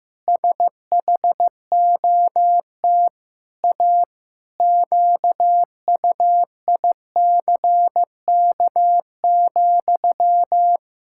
Morse code, heard as a keyed tone, about 15 wpm, 700 Hz. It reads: SHOT A QUICK,
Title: Pangram Challenge v2 - 15wpm